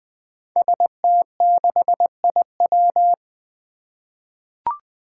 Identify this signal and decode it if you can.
Morse code, heard as a keyed tone, about 20 wpm, 700 Hz. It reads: ST6IW
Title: Five Challenging Letter-Number Combinations 20wpm